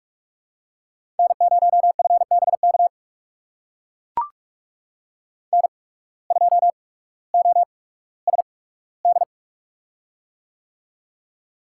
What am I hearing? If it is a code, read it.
Morse code, heard as a keyed tone, about 44 wpm, spaced as for 19 wpm, 700 Hz. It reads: N0FBK N2OSD
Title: USA Call Signs - 5 Characters - 8x Character Spacing - 45wpm